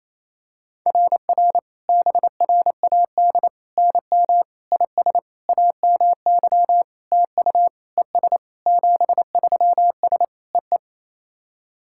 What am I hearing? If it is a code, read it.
Morse code, heard as a keyed tone, about 27 wpm, 690 Hz. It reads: RR BRAD NM IS AMY TU ES 73S EE
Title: SOTA QSO Practice 28wpm